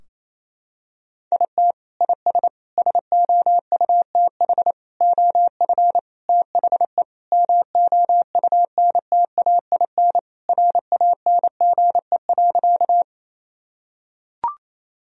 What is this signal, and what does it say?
Morse code, heard as a keyed tone, about 28 wpm, 700 Hz. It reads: IT IS SOUTH OF THE MOUNTAIN RANGE.
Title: Sentences from Top 1000 Words - Review Part 4 - 28wpm